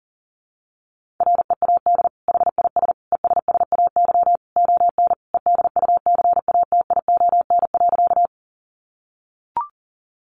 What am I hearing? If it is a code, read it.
Morse code, heard as a keyed tone, about 40 wpm, 700 Hz. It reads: READ HIS ESSAY ON EDUCATION.
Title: Sentences from Top 1500 Words 40wpm